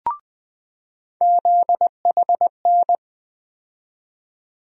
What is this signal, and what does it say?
Morse code, heard as a keyed tone, about 20 wpm, 700 Hz. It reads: ZHN
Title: USA Call Sign Postfixes - 3 Letters 20wpm